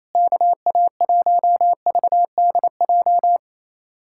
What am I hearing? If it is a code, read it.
Morse code, heard as a keyed tone, about 28 wpm, 700 Hz. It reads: KA1VDJ
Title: USA Callsign DE Callsign - 6 Characters - 28wpm